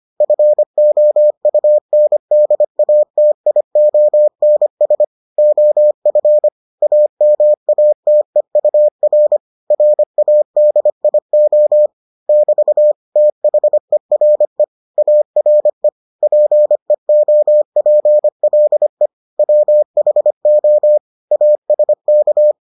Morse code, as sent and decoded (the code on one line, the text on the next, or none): ..-. --- ..- -. -.. .- - .. --- -. ... / --- ..-. / .- -- .- - . ..- .-. / .-. .- -.. .. --- / -...- / - .... .-. / .- .-. / .--. --- .--. .-.. / .-- .... --- / .- ... -.-
FOUNDATIONS OF AMATEUR RADIO <BT> THERE ARE PEOPLE WHO ASK